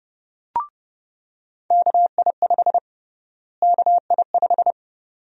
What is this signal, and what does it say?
Morse code, heard as a keyed tone, about 30 wpm, 700 Hz. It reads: KI5 KI5
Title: USA Call Sign Prefixes - Repeated 2x - 30wpm